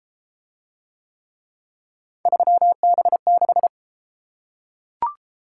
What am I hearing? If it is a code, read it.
Morse code, heard as a keyed tone, about 28 wpm, 700 Hz. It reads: 3B6